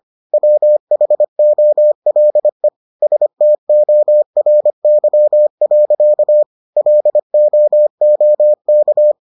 Morse code, heard as a keyed tone, about 25 wpm, 600 Hz. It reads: WHOLE STORY. LOOK